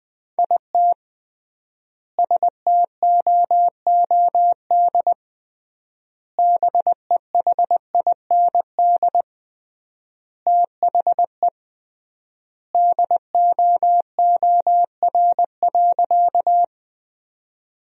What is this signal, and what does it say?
Morse code, heard as a keyed tone, about 20 wpm, 700 Hz. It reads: IT STOOD BEHIND THE DOOR.
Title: 3x Word-Spacing - Sentences from Top 500 Words - 20wpm